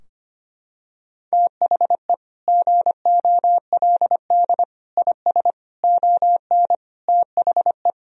Morse code, heard as a keyed tone, about 25 wpm, 700 Hz. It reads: THE GOLD IS ON THE
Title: Sentences from Top 1000 Words - Review Part 3 - 25wpm